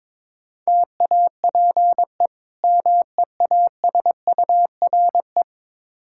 Morse code, heard as a keyed tone, about 22 wpm, 700 Hz. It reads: TAPE MEASURE
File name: practice / 22wpm / Difficult Words with Hints - 22wpm